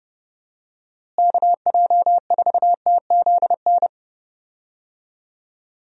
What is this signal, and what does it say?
Morse code, heard as a keyed tone, about 30 wpm, 700 Hz. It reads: KJ4TZN
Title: USA Call Signs - 6 Characters - Repeated 2x - 30wpm